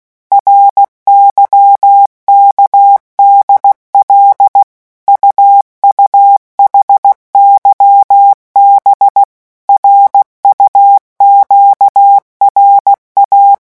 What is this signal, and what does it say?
Morse code, heard as a keyed tone, about 16 wpm, 800 Hz. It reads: RYKDL UUHYB RUQRA
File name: morse mp3 / revision5-vitesse 15 mots